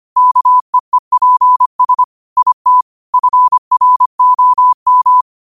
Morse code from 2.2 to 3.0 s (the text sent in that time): IT